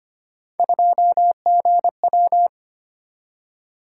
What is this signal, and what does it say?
Morse code, heard as a keyed tone, about 25 wpm, 700 Hz. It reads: 2GW
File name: Three Letter-Number Combinations - 25wpm